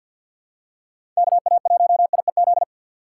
Morse code, heard as a keyed tone, about 50 wpm, 700 Hz. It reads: KA1IEZ
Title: USA Call Signs - 6 Characters - 3x Character Spacing - 50wpm